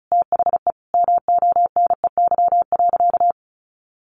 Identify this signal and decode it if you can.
Morse code, heard as a keyed tone, about 34 wpm, 700 Hz. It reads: THE MONEY.
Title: Sentences from Top 1500 Words 35wpm